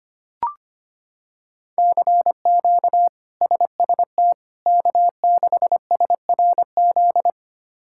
Morse code, heard as a keyed tone, about 25 wpm, 700 Hz. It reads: CQ SST K6SRZ
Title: K1USN SST QSO Practice - 25wpm